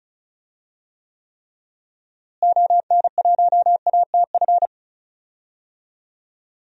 Morse code, heard as a keyed tone, about 36 wpm, 700 Hz. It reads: ON1ATF
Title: DXCC-100 Call Signs - 6 Characters - Repeated 3x - 35wpm